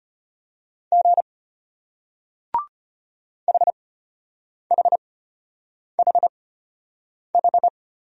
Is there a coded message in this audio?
Yes: Morse code from 0.9 to 7.7 s, heard as a keyed tone, about 33 wpm, spaced as for 8 wpm, 700 Hz. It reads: G HHHH